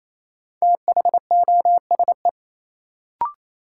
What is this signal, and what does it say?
Morse code, heard as a keyed tone, about 28 wpm, 700 Hz. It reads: THOSE
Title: Sentences from Top 300 Words - New Word Drill - 28wpm